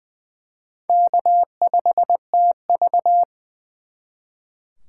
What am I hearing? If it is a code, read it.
Morse code, heard as a keyed tone, about 24 wpm, 700 Hz. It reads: K5TV
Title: Speed Racing - USA Call Signs 20wpm